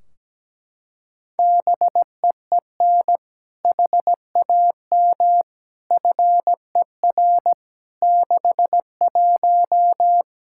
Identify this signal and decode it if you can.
Morse code, heard as a keyed tone, about 17 wpm, 700 Hz. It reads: BEEN HAM FER 61
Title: Practice QSO - Exchange 3 - More Info - 17wpm